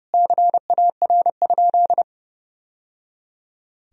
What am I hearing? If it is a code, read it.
Morse code, heard as a keyed tone, about 30 wpm, 700 Hz. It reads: CAR?
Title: Sentences from Top 1000 Words - Review Part 1 - 30wpm